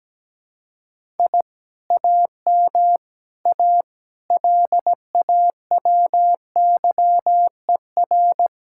Morse code, heard as a keyed tone, about 17 wpm, 700 Hz. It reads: I AM A LAWYER
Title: Speed-Racing Common QSO Phrases Encore - 17wpm